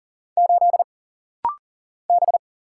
Morse code, heard as a keyed tone, about 40 wpm, 700 Hz. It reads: 8 B
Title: Single Letter-Number Alt - Rapid-Fire 40wpm